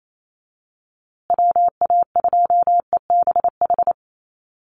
Morse code, heard as a keyed tone, about 28 wpm, 700 Hz. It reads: WA2EBH
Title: USA Call Signs Repeated 2x 28wpm